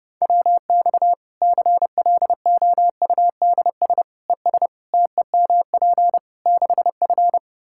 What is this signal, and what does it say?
Morse code, heard as a keyed tone, about 29 wpm, 700 Hz. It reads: WX CLOUDS ES TEMP 6F
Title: Practice QSO - Exchange 2 - Info - 30wpm